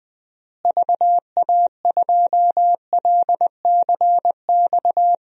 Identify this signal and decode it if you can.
Morse code, heard as a keyed tone, about 21 wpm, 700 Hz. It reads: VA2LCX
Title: DXCC-100 Call Signs - 6 Characters - Repeated 2x - 20wpm